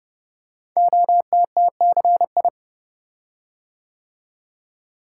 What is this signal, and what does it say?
Morse code, heard as a keyed tone, about 29 wpm, 700 Hz. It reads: OTTCI
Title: Code Groups - 5 Letters - 30wpm